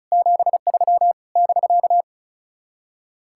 Morse code, heard as a keyed tone, about 45 wpm, 700 Hz. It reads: 73 <BK>